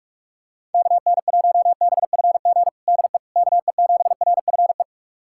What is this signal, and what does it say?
Morse code, heard as a keyed tone, about 45 wpm, 700 Hz. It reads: KN1BFG DE KE7RFE
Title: USA Callsign DE Callsign - 6 Characters - 45wpm